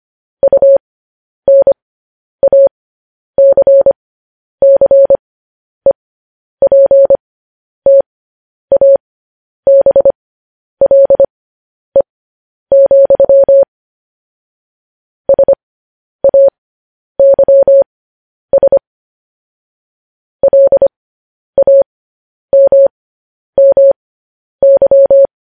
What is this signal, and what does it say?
Morse code, heard as a keyed tone, about 25 wpm, spaced as for 10 wpm, 550 Hz. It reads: UNACCEPTABLE, SAYS LAMMY
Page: news headlines in morse code at 10 wpm